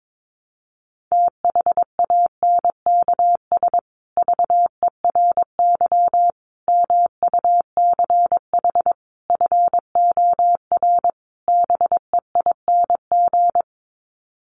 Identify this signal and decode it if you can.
Morse code, heard as a keyed tone, about 22 wpm, 700 Hz. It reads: THANKS VERY MUCH FOR BEING
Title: Sets of 5 Words Encore 22wpm